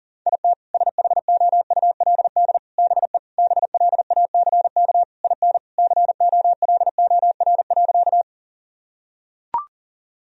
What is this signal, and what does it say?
Morse code, heard as a keyed tone, about 40 wpm, 700 Hz. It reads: IT SHOULD BE BLACK IN COLOR.
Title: Sentences from Top 1000 Words - Review Part 2 - 40wpm